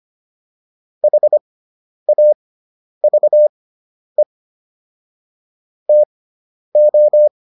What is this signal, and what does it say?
Morse code, heard as a keyed tone, about 25 wpm, spaced as for 10 wpm, 600 Hz. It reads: HAVE TO
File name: WD401 10